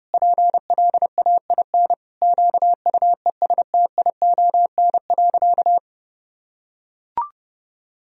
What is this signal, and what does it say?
Morse code, heard as a keyed tone, about 32 wpm, 700 Hz. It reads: PLAIN QUESTION.
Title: Sentences from Top 1000 Words - Review Part 2 - 30wpm